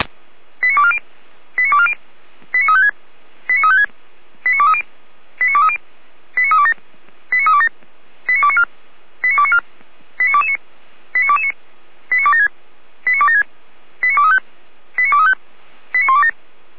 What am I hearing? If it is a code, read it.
Five-tone selective call (ZVEI), heard as quick runs of five tones, tones of about 70 ms. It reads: 89240 89240 89356 89356 89230 89230 89327 89327 89294 89294 89229 89229 89276 89276 89235 89235 89127
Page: SDR/ZVEI